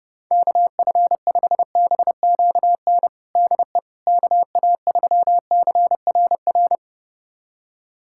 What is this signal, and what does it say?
Morse code, heard as a keyed tone, about 30 wpm, 700 Hz. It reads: KF5BQN DE KA3CRR
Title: USA Callsign DE Callsign - 6 Characters - 30wpm